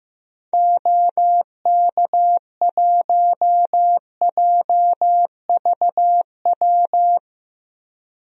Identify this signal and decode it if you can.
Morse code, heard as a keyed tone, about 15 wpm, 700 Hz. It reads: OK1JVW
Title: DXCC-100 Call Signs - 6 Characters - 15wpm